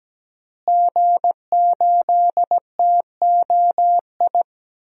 Morse code, heard as a keyed tone, about 17 wpm, 700 Hz. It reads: G8TOI
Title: DXCC-100 Call Signs - 5 Characters - Repeated 3x - 17wpm